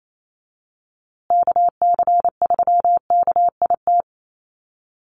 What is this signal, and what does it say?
Morse code, heard as a keyed tone, about 28 wpm, 700 Hz. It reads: KC3KIT
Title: USA Call Signs Repeated 3x 28wpm